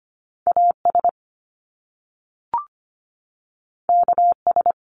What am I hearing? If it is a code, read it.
Morse code, heard as a keyed tone, about 25 wpm, 700 Hz. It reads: A S K S
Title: Top Two Letter Combinations in Words 25wpm